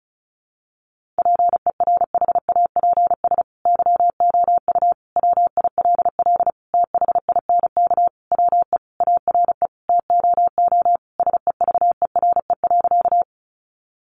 Morse code, heard as a keyed tone, about 34 wpm, 700 Hz. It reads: PERHAPS YOU WILL THINK WE ARE TOO SEVERE.
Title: Sentences from Top 1400 Words 35wpm